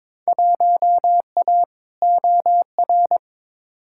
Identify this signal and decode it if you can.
Morse code, heard as a keyed tone, about 22 wpm, 700 Hz. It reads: 1A OR
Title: Field Day Exchanges - 22wpm